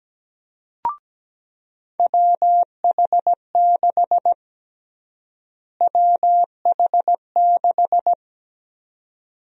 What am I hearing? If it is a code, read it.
Morse code, heard as a keyed tone, about 17 wpm, 700 Hz. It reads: WH6 WH6
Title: USA Call Sign Prefixes - Repeated 2x - 17wpm